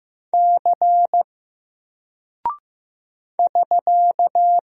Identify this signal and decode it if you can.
Morse code, heard as a keyed tone, about 15 wpm, 700 Hz. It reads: C <SK>
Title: Single Character (Letter:Number:Punctuation:Prosign) Alt 15wpm